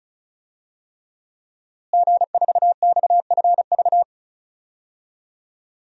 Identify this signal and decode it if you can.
Morse code, heard as a keyed tone, about 34 wpm, 700 Hz. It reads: G4XFV
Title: DXCC-100 Call Signs - 5 Characters - Repeated 3x - 35wpm